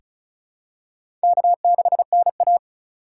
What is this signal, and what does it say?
Morse code, heard as a keyed tone, about 36 wpm, 700 Hz. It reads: K6NA
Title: K1USN SST QSO Practice - 35wpm